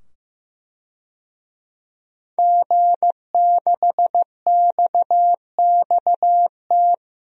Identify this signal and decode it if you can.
Morse code, heard as a keyed tone, about 15 wpm, 700 Hz. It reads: G6XXT
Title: DXCC-100 Call Signs - 5 Characters - 15wpm